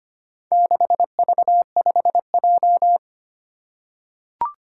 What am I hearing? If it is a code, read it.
Morse code, heard as a keyed tone, about 25 wpm, 700 Hz. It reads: 6V5J
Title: Four Challenging Letter-Number Combinations 25wpm